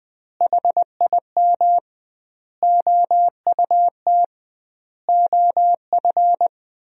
Morse code, heard as a keyed tone, about 20 wpm, 700 Hz. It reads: HIM OUT OF
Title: 2x Word-Spacing - Sentences from Top 300 Words - 20wpm